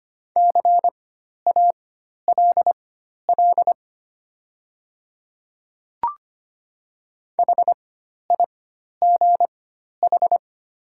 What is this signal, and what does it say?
Morse code, heard as a keyed tone, about 25 wpm, spaced as for 12 wpm, 700 Hz. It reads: CALL HIGH
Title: 4x Character Spacing - Top 200 Words - 25wpm